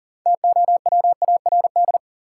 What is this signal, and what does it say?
Morse code, heard as a keyed tone, about 40 wpm, 700 Hz. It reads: TOWARD
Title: Sentences from Top 1000 Words - Review Part 6 - New Word Drill - 40wpm